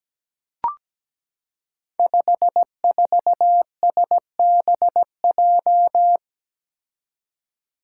Morse code, heard as a keyed tone, about 17 wpm, 700 Hz. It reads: 54SBJ
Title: Five Challenging Letter-Number Combinations 17wpm